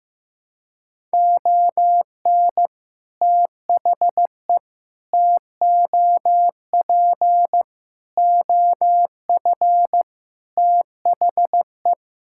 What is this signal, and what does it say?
Morse code, heard as a keyed tone, about 15 wpm, 700 Hz. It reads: ON THE TOP OF THE
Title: Sentences from Top 1000 Words - Review Part 6 - 15wpm